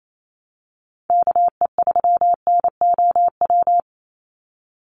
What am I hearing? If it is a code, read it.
Morse code, heard as a keyed tone, about 28 wpm, 700 Hz. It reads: KE3NOW
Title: USA Call Signs 28wpm